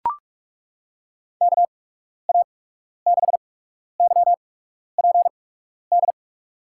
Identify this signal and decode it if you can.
Morse code, heard as a keyed tone, about 41 wpm, spaced as for 12 wpm, 700 Hz. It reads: KA6YPD
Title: USA Call Signs - 6 Characters - 8x Character Spacing - 45wpm